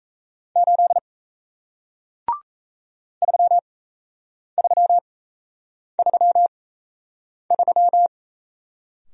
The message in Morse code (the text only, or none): 8 3333